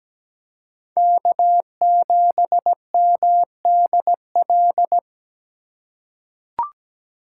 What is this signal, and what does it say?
Morse code, heard as a keyed tone, about 17 wpm, 700 Hz. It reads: K7MDL
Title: USA Call Signs - 5 Characters - 4x Character Spacing - 17wpm